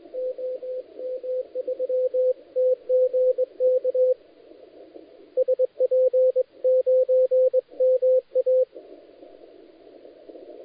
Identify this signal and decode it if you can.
Morse code, heard as a keyed tone, about 19 wpm, 510 Hz, with radio noise. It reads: OM3TGK SP9MA
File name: dlazaawansowanych / nagraniazpasma / titawaNETcontrol